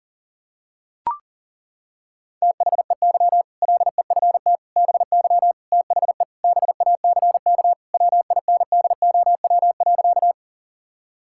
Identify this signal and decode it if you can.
Morse code, heard as a keyed tone, about 40 wpm, 700 Hz. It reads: THEY LEFT BY THE BACK WINDOW.
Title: Sentences from Top 1600 Words 40wpm